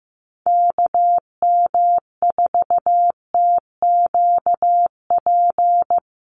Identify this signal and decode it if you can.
Morse code, heard as a keyed tone, about 15 wpm, 700 Hz. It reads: KM4TQP
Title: USA Call Signs 15wpm